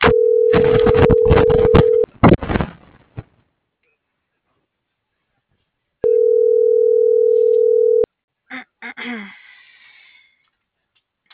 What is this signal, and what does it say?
Ringback tone (North America), 440 + 480 Hz.